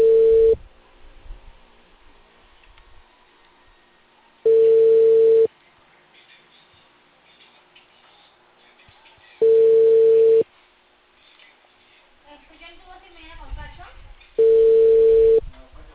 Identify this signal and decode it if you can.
Ringback tone (China, 450 Hz), with speech.